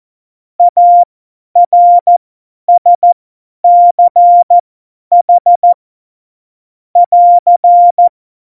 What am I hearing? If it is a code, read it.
Morse code, heard as a keyed tone, about 14 wpm, spaced as for 10 wpm, 700 Hz. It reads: ARSCH <AR>
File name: Morsekurs / Witze / Witz 25. September 2018, 50 F70